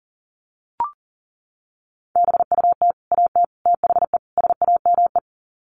Morse code, heard as a keyed tone, about 40 wpm, 700 Hz. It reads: BUT AT THE SAME
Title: Sets of 4 Words 40wpm